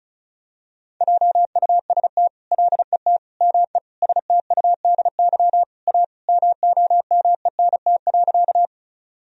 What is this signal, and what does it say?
Morse code, heard as a keyed tone, about 34 wpm, 700 Hz. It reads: JUST LET ME STUDY A MOMENT.
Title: Sentences from Top 1000 Words - Review Part 1 - 35wpm